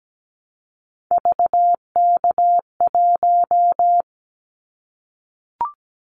Morse code, heard as a keyed tone, about 17 wpm, 700 Hz. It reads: VK1
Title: Callsign Prefixes for Top 50 DXCC - 17wpm